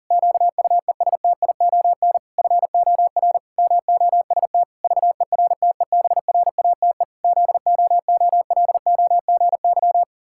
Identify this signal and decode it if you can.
Morse code, heard as a keyed tone, about 40 wpm, 700 Hz. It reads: QUESTION FOR MOST VERTEBRATE ZOOLOGY
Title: Pangram Challenge v2 - 40wpm